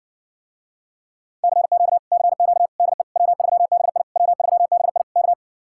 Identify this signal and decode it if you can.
Morse code, heard as a keyed tone, about 60 wpm, 700 Hz. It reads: CQ CQ DE W3BI W3BI K